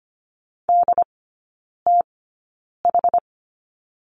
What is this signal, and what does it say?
Morse code, heard as a keyed tone, about 25 wpm, spaced as for 9 wpm, 700 Hz. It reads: DTH
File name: course / 25:10wpm / 093 Lesson 16e - Letter c - Three Characters 25wpm